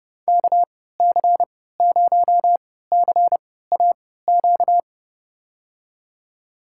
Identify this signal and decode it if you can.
Morse code, heard as a keyed tone, about 30 wpm, spaced as for 17 wpm, 700 Hz. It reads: KC0CAQ
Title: USA Call Signs - 6 Characters - 3x Character Spacing - 30wpm